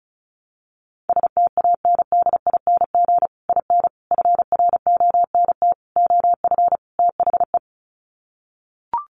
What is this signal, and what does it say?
Morse code, heard as a keyed tone, about 34 wpm, 700 Hz. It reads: STANDING IN FRONT OF THE
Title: Sets of 5 Words Encore2 35wpm